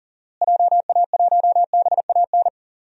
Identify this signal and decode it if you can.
Morse code, heard as a keyed tone, about 40 wpm, 700 Hz. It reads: JA1BAN
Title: DXCC-100 Call Signs - 6 Characters - Repeated 2x - 40wpm